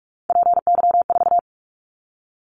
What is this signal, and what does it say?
Morse code, heard as a keyed tone, about 46 wpm, 700 Hz. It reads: PY4
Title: Callsign Prefixes for Top 100 DXCC - 45wpm